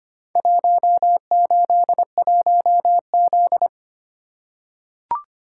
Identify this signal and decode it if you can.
Morse code, heard as a keyed tone, about 25 wpm, 690 Hz. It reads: 181Z
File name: practice / 25wpm / Four Letter-Number Combinations - 25wpm